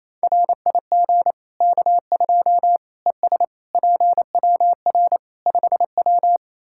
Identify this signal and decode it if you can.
Morse code, heard as a keyed tone, about 28 wpm, 700 Hz. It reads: RIG K2 ES PWR 5W
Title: Practice QSO - Exchange 2 - Info - 28wpm